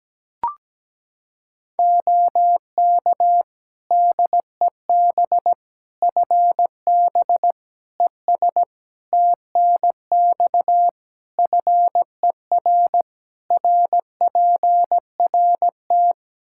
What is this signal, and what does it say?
Morse code, heard as a keyed tone, about 17 wpm, 700 Hz. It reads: OK DEB FB ES TNX FER RPRT
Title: Practice QSO - Exchange 2 - Info - 17wpm